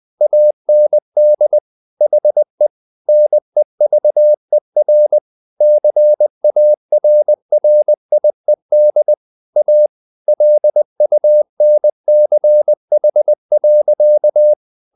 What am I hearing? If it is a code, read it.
Morse code, heard as a keyed tone, about 20 wpm, 600 Hz. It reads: AND HE NEVER CARRIED A LUNCH.
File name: the old man and the sea / the old man and the sea part 1 20wpm